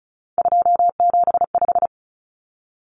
Morse code, heard as a keyed tone, about 34 wpm, 700 Hz. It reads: 275
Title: Common Cut Numbers - 3-Digit 35wpm